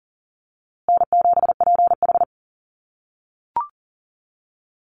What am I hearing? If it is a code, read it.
Morse code, heard as a keyed tone, about 40 wpm, 700 Hz. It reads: N7PH
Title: USA Call Signs Repeated 3x 40wpm